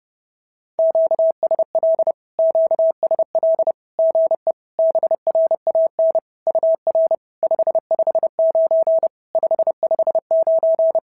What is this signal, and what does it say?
Morse code, heard as a keyed tone, about 30 wpm, 650 Hz. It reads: QSL QSL GE BRAN UR 559 559
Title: SOTA QSO Practice 30wpm